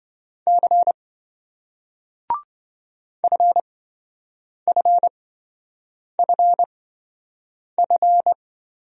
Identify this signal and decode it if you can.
Morse code, heard as a keyed tone, about 28 wpm, spaced as for 7 wpm, 700 Hz. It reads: C FFFF